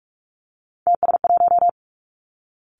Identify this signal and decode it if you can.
Morse code, heard as a keyed tone, about 46 wpm, 700 Hz. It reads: TS1
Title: Three Challenging Letter-Number Combinations 45wpm